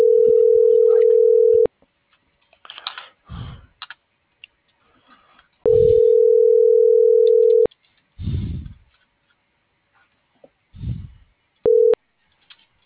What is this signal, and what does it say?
Ringback tone (North America), 440 + 480 Hz.